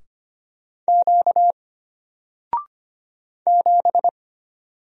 Morse code, heard as a keyed tone, about 25 wpm, 700 Hz. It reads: Q 7
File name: Single-Letter-Number-25wpm